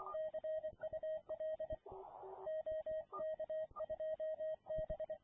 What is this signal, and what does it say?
Morse code, heard as a keyed tone, about 24 wpm, 630 Hz. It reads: CUL OK2B